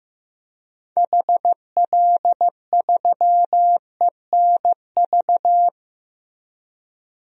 Morse code, heard as a keyed tone, about 15 wpm, 700 Hz. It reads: HL3ENV
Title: DXCC-100 Call Signs - 6 Characters - 15wpm